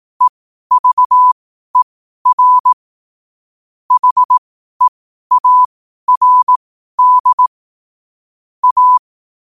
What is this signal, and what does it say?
Morse code, heard as a keyed tone, about 18 wpm, spaced as for 12 wpm, 1000 Hz. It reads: EVER HEARD A
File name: qotdfiles / 12wpm